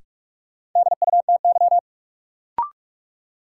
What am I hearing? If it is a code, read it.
Morse code, heard as a keyed tone, about 46 wpm, 700 Hz. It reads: DUTY
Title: Sentences from Top 1500 Words - New Word Drill Alt - 45wpm